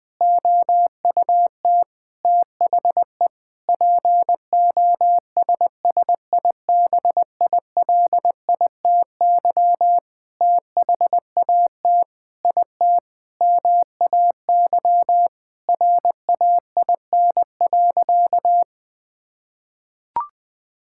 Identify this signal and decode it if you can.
Morse code, heard as a keyed tone, about 20 wpm, 700 Hz. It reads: OUT THE POSSIBILITY THAT IT MAY RAIN.
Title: Sentences with 8-Word Phrases - 20wpm